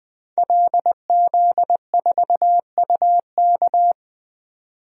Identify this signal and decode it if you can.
Morse code, heard as a keyed tone, about 21 wpm, 700 Hz. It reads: LZ4UK
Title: DXCC-100 Call Signs - 5 Characters - Repeated 3x - 20wpm